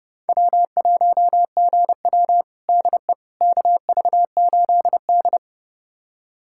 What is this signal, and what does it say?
Morse code, heard as a keyed tone, about 30 wpm, 700 Hz. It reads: W1GW DE KV8D
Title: USA Callsign DE Callsign - 4 Characters - 30wpm